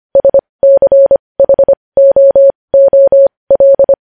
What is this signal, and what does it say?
Morse code, heard as a keyed tone, about 25 wpm, spaced as for 20 wpm, 550 Hz. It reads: SCHOOL
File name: podcast / News headlines-1x-25x20